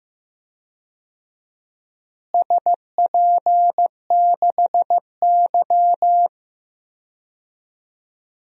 Morse code, heard as a keyed tone, about 15 wpm, 700 Hz. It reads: SP6Y
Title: DXCC-100 Call Signs - 4 Characters - Repeated 2x - 15wpm